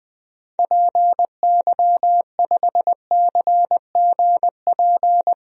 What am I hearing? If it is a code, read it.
Morse code, heard as a keyed tone, about 20 wpm, 700 Hz. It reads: PY5CGP
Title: DXCC-100 Call Signs - 6 Characters - Repeated 3x - 20wpm